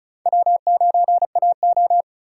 Morse code, heard as a keyed tone, about 44 wpm, 700 Hz. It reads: W9AO
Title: Speed Racing - USA 1x2 Call Signs 35wpm